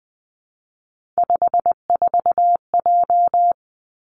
Morse code, heard as a keyed tone, about 20 wpm, 700 Hz. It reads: 54J
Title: Three Challenging Letter-Number Combinations 20wpm